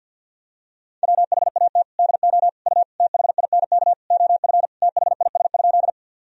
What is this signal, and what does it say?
Morse code, heard as a keyed tone, about 48 wpm, 700 Hz. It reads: WHAT DO U THINK OF THIS?
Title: Sentences from Top 200 Words and Abbreviations 50wpm